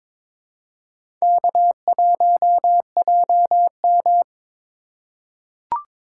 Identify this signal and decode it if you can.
Morse code, heard as a keyed tone, about 21 wpm, 700 Hz. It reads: K1JM
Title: USA Call Signs - 4 Characters - 4x Character Spacing - 22wpm